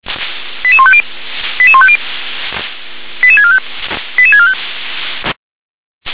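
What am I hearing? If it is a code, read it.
Five-tone selective call (ZVEI), heard as quick runs of five tones, tones of about 70 ms. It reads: 88150 88150 88645 88645
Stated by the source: ZVEI five-tone selective call received on SDR.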